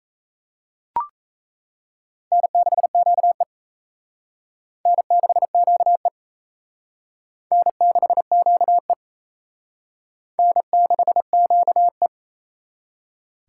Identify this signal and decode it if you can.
Morse code, heard as a keyed tone, about 34 wpm, 700 Hz. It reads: N6QE N6QE N6QE N6QE
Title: Speed Racing - USA 1x2 Call Signs 28wpm